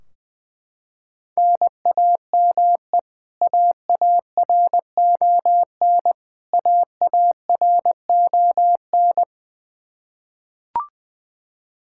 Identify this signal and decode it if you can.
Morse code, heard as a keyed tone, about 20 wpm, 700 Hz. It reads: NAME AARON AARON
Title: Practice QSO - Exchange 1 - Beginning - 20wpm